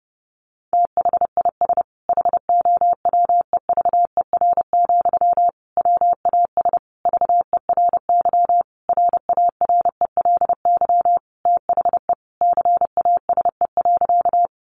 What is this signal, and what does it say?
Morse code, heard as a keyed tone, about 30 wpm, 700 Hz. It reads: THIS HOWEVER, WAS VERY RARELY THE CASE.
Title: Sentences from Top 1400 Words 30wpm